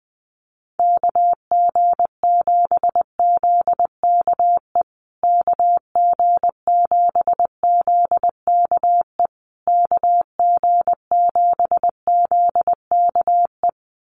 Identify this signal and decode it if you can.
Morse code, heard as a keyed tone, about 19 wpm, 700 Hz. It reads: KG7ZKE KG7ZKE KG7ZKE
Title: USA Call Signs Repeated 3x 20wpm